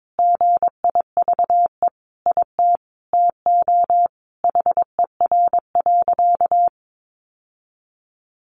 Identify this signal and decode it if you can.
Morse code, heard as a keyed tone, about 22 wpm, 700 Hz. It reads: GIVE IT TO HER.
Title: Sentences from Top 1400 Words 22wpm